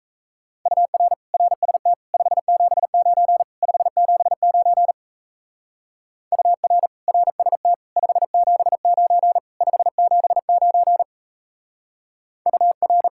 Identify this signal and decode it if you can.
Morse code, heard as a keyed tone, about 34 wpm, 700 Hz. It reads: UR RST 579 579 UR RST 579 579 UR
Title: Speed-Racing Common QSO Phrases - 28wpm